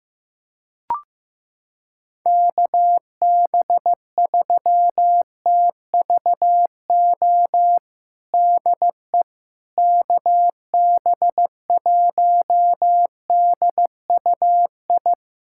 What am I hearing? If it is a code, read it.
Morse code, heard as a keyed tone, about 15 wpm, 700 Hz. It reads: KB3TVO DE KB1DUI
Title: USA Callsign DE Callsign - 6 Characters - 15wpm